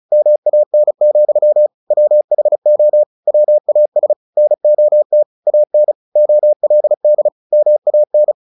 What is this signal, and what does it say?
Morse code, heard as a keyed tone, about 34 wpm, 600 Hz. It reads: MAN, WHO WAS NOT AN OLD MAN
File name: the old man and the sea 2 35wpm